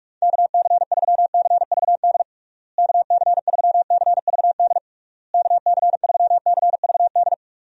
Morse code, heard as a keyed tone, about 43 wpm, 700 Hz. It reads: KC3CVD KC3CVD KC3CVD
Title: USA Call Signs - 6 Characters - Repeated 3x - 45wpm